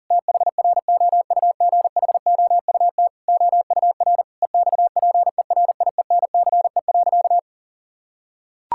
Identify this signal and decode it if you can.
Morse code, heard as a keyed tone, about 40 wpm, 700 Hz. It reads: THROUGHOUT OUR EXPERIENCE.
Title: Sentences from Top 1000 Words - Review Part 5 - 40wpm